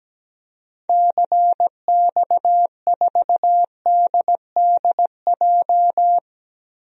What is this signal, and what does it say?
Morse code, heard as a keyed tone, about 17 wpm, 700 Hz. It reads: CX4DDJ
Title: DXCC-100 Call Signs - 6 Characters - 17wpm